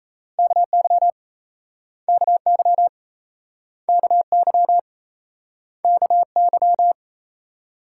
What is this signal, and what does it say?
Morse code, heard as a keyed tone, about 38 wpm, 700 Hz. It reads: KY KY KY KY